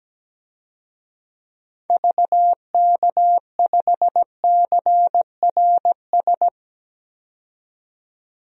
Morse code, heard as a keyed tone, about 17 wpm, 700 Hz. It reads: VK5CRS
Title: DXCC-100 Call Signs - 6 Characters - 17wpm